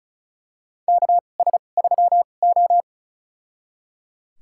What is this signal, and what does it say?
Morse code, heard as a keyed tone, about 36 wpm, spaced as for 25 wpm, 700 Hz. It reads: KS3O
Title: USA Call Signs - 4 Characters - 2x Character Spacing - 35wpm